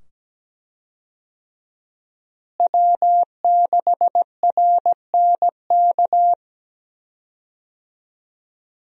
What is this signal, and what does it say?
Morse code, heard as a keyed tone, about 17 wpm, 700 Hz. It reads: W6RNK
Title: USA Call Signs - 5 Characters - Repeated 2x - 17wpm